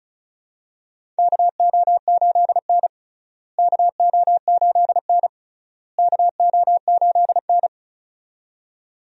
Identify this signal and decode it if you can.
Morse code, heard as a keyed tone, about 36 wpm, 700 Hz. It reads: KO8N KO8N KO8N
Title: USA Call Signs - 4 Characters - Repeated 3x - 35wpm